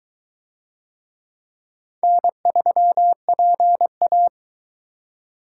Morse code, heard as a keyed tone, about 21 wpm, 700 Hz. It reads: N3PA